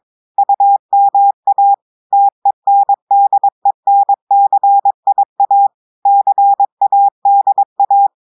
Morse code, heard as a keyed tone, about 22 wpm, 800 Hz. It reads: UMA TENDENCIA CADA
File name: audios / portugais / portugais texte en clair 2 vitesse 22 mots minutes